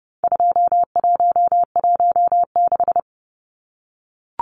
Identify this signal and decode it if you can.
Morse code, heard as a keyed tone, about 28 wpm, 700 Hz. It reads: 2116
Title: Numbers - 4-Digit 30wpm